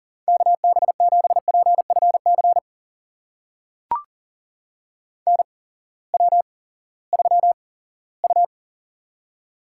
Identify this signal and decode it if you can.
Morse code, heard as a keyed tone, about 37 wpm, spaced as for 19 wpm, 700 Hz. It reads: KB7PFC NW3U
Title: USA Call Signs - 8x Character Spacing - 40wpm